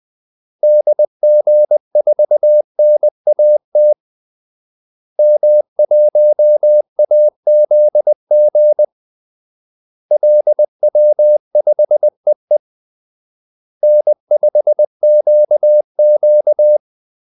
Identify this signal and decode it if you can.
Morse code, heard as a keyed tone, about 20 wpm, 600 Hz. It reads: DG4NAT M1AZG LW5EE N5QQ
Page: Kaldesignaler 20 wpm